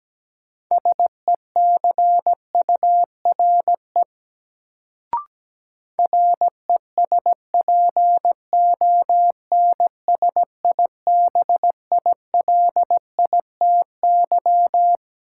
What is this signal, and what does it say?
Morse code, heard as a keyed tone, about 17 wpm, 700 Hz. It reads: SECURE RESPONSIBILITY
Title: Sentences from Top 1200 Words - New Word Drill - 17wpm